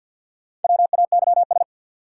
Morse code, heard as a keyed tone, about 48 wpm, 700 Hz. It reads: WAYS
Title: Sets of 5 Words Encore - New Word Drill - 50wpm